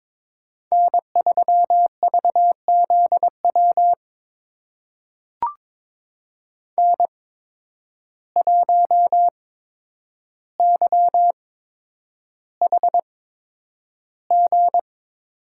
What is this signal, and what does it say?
Morse code, heard as a keyed tone, about 22 wpm, spaced as for 9 wpm, 700 Hz. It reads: N3VZW N1YHG
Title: USA Call Signs - 5 Characters - 8x Character Spacing - 22wpm